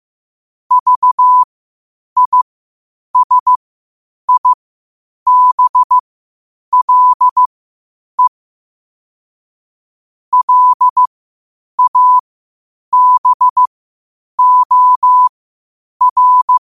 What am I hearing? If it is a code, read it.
Morse code, heard as a keyed tone, about 15 wpm, spaced as for 8 wpm, 1000 Hz. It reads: VISIBLE LABOR
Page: qotd-at-8-wpm